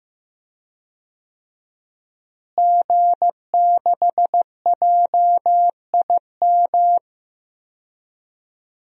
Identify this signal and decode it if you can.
Morse code, heard as a keyed tone, about 15 wpm, 700 Hz. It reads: G6JIM
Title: DXCC-100 Call Signs - 5 Characters - Repeated 2x - 15wpm